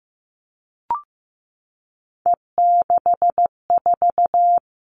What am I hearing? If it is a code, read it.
Morse code, heard as a keyed tone, about 15 wpm, 700 Hz. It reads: E64
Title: All Cut Numbers - 3-Digit 15wpm